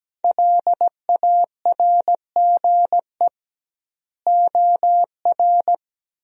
Morse code, heard as a keyed tone, about 17 wpm, 700 Hz. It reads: LARGE OR
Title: 2x Word-Spacing - Sentences from Top 200 Words - 17wpm